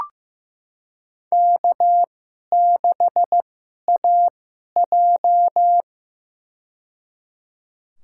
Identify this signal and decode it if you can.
Morse code, heard as a keyed tone, about 15 wpm, spaced as for 11 wpm, 700 Hz. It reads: K6AJ
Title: USA Call Signs - 4 Characters - 2x Character Spacing - 15wpm